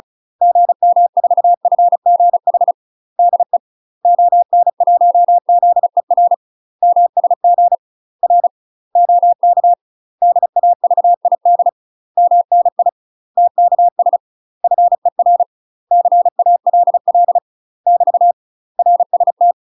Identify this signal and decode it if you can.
Morse code, heard as a keyed tone, about 34 wpm, 700 Hz. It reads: GM4FGH DE ON1ZER MSG R OK DAVID MNI TKS FER CALL <BT> RST
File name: UK MASTER QSOs TRAINING / 35WPM / UK MASTER QSOs 0025